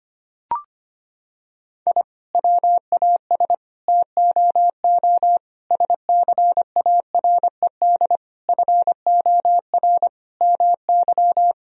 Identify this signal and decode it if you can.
Morse code, heard as a keyed tone, about 25 wpm, 700 Hz. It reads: I WAS TOO SCARED FOR MY
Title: Sentences from Top 1000 Words - Review Part 4 - 25wpm